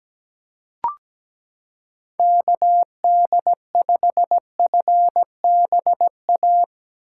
Morse code, heard as a keyed tone, about 17 wpm, 700 Hz. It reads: KD5FBA
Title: K1USN SST QSO Practice - 17wpm